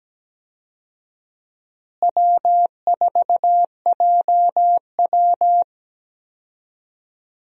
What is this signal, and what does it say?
Morse code, heard as a keyed tone, about 17 wpm, 700 Hz. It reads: W4JW
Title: USA Call Signs - 4 Characters - Repeated 3x - 17wpm